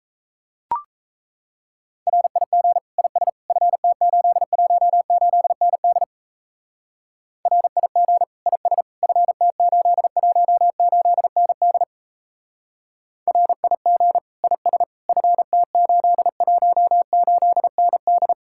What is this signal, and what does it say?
Morse code, heard as a keyed tone, about 34 wpm, 700 Hz. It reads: RIG IS FT818ND RIG IS FT818ND RIG IS FT818ND
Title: Speed-Racing Common QSO Phrases Encore - 28wpm